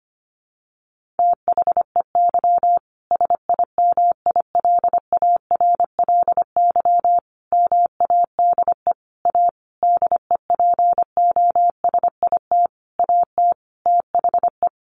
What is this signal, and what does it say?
Morse code, heard as a keyed tone, about 25 wpm, 700 Hz. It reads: THEY SIMILARLY MADE A DEPOSIT AT THE
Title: Sentences from Top 1400 Words 25wpm